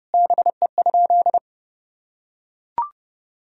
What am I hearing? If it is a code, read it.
Morse code, heard as a keyed tone, about 30 wpm, 700 Hz. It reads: BE?
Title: Sentences from Top 1000 Words - Review Part 4 - 30wpm